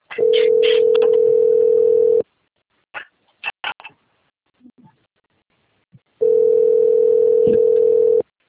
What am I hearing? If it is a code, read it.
Ringback tone (North America), 440 + 480 Hz.